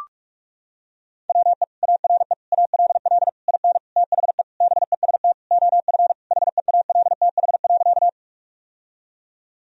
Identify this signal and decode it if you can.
Morse code, heard as a keyed tone, about 42 wpm, 700 Hz. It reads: WE ARE ALL IN THE BEST OF HEALTH.
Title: Sentences from Top 1000 Words - Review Part 3 - 45wpm